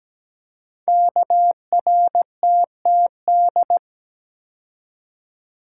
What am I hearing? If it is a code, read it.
Morse code, heard as a keyed tone, about 17 wpm, 700 Hz. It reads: KRTTD
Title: Code Groups - 5 Letter:Numbers - 17wpm